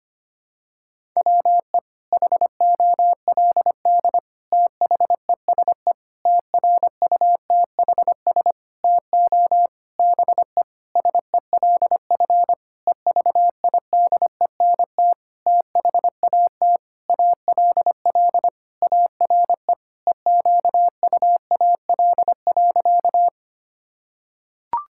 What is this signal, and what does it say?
Morse code, heard as a keyed tone, about 25 wpm, 700 Hz. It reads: WE HOLD THESE TRUTHS TO BE SELF EVIDENT THAT ALL ARE EQUAL.
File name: practice / 25wpm / Sentences with 9-Word Phrases - 25wpm